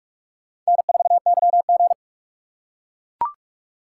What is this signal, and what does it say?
Morse code, heard as a keyed tone, about 41 wpm, 700 Hz. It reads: N4YG
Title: USA Call Signs - 4x Character Spacing - 45wpm